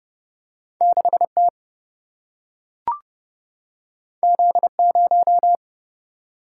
Morse code, heard as a keyed tone, about 30 wpm, 700 Hz. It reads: 6T Z0
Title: Single-Letter Single-Number 30wpm